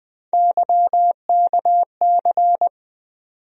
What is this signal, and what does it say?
Morse code, heard as a keyed tone, about 20 wpm, 700 Hz. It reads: YKC
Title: USA Call Sign Postfixes - 3 Letters 20wpm